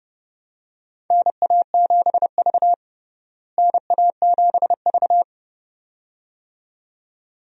Morse code, heard as a keyed tone, about 32 wpm, 700 Hz. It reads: NA7V NA7V
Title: USA Call Signs - 4 Characters - Repeated 2x - 30wpm